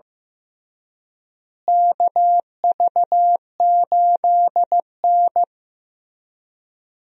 Morse code, heard as a keyed tone, about 15 wpm, 700 Hz. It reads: KV8N
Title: USA Call Signs - 4 Characters - Repeated 3x - 15wpm